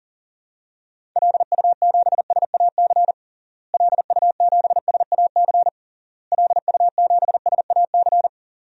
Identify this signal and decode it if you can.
Morse code, heard as a keyed tone, about 37 wpm, 700 Hz. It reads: LU7SAC LU7SAC LU7SAC
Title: DXCC-100 Call Signs - 6 Characters - Repeated 3x - 40wpm